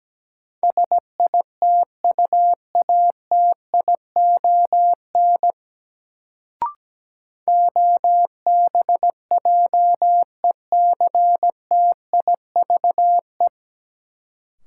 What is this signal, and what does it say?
Morse code, heard as a keyed tone, about 17 wpm, 700 Hz. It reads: SITUATION OBJECTIVE
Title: Sentences from Top 1200 Words - New Word Drill Alt - 17wpm